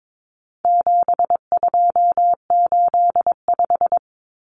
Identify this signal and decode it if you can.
Morse code, heard as a keyed tone, about 22 wpm, 700 Hz. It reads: 7285
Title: Common Cut Numbers - 4-Digit 22wpm